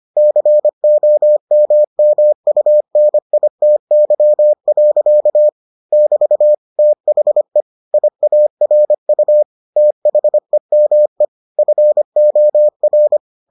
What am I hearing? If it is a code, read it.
Morse code, heard as a keyed tone, about 25 wpm, 600 Hz. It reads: COMMUNITY. <BT> THE IARU THEME FOR